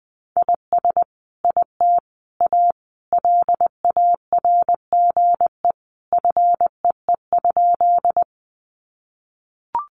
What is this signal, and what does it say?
Morse code, heard as a keyed tone, about 20 wpm, 700 Hz. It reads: IS IT A LARGE FEE?
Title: Sentences from Top 1500 Words 20wpm